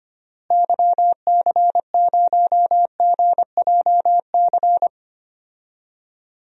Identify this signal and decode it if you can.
Morse code, heard as a keyed tone, about 25 wpm, 700 Hz. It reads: YC0GJC
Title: DXCC-100 Call Signs - 6 Characters - Repeated 3x - 25wpm